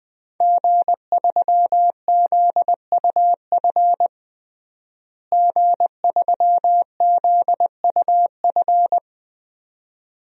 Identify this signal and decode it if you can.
Morse code, heard as a keyed tone, about 20 wpm, 700 Hz. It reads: G3ZUF G3ZUF
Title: DXCC-100 Call Signs - 5 Characters - Repeated 2x - 20wpm